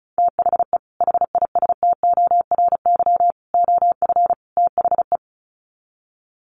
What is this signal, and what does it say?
Morse code, heard as a keyed tone, about 34 wpm, 700 Hz. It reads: THE HISTORY OF THE
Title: Sets of 4 Words 35wpm